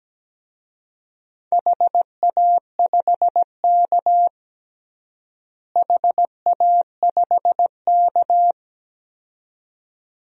Morse code, heard as a keyed tone, about 17 wpm, 700 Hz. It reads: HA5K HA5K
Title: DXCC-100 Call Signs - 4 Characters - Repeated 2x - 17wpm